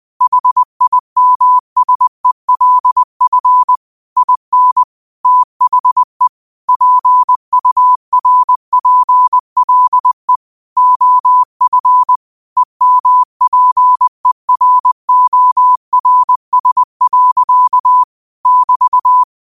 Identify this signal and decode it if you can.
Morse code, heard as a keyed tone, about 20 wpm, 1000 Hz. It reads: HIMSELF IN THE PURPLE OF EMPERORS. <BT>